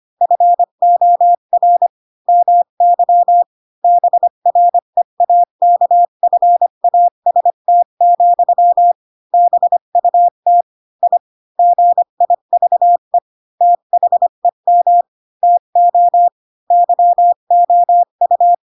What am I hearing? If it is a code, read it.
Morse code, heard as a keyed tone, about 25 wpm, 700 Hz. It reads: FOR MY BREAKFAST, BUT I GIVE THEM TO YOU
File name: Pinocchio / Pinocchio ch7 20 25 ik1hge